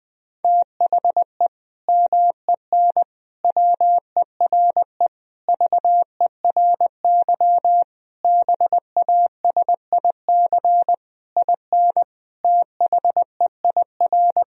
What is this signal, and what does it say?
Morse code, heard as a keyed tone, about 21 wpm, 700 Hz. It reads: THE MEN WERE VERY BASIC IN THEIR
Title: Sentences from Top 1000 Words - Review Part 3 - 20wpm